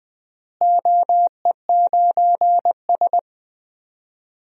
Morse code, heard as a keyed tone, about 20 wpm, 700 Hz. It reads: OE9S
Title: DXCC-100 Call Signs - 4 Characters - Repeated 3x - 20wpm